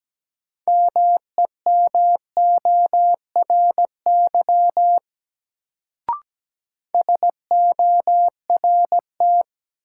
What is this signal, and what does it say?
Morse code, heard as a keyed tone, about 17 wpm, 700 Hz. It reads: MEMORY SORT
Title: Sentences from Top 1300 Words - New Word Drill Alt - 17wpm